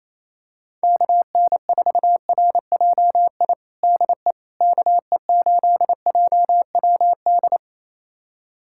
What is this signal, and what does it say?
Morse code, heard as a keyed tone, about 28 wpm, 700 Hz. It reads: KN4RJI DE KE8JWD
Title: USA Callsign DE Callsign - 6 Characters - 28wpm